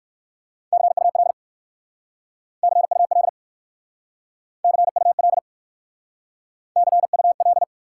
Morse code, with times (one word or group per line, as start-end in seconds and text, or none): 0.7-1.3 CUL
2.6-3.3 CUL
4.6-5.4 CUL
6.8-7.6 CUL